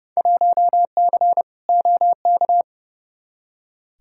Morse code, heard as a keyed tone, about 30 wpm, 700 Hz. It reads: 1C OK
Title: Field Day Exchanges Encore - 30wpm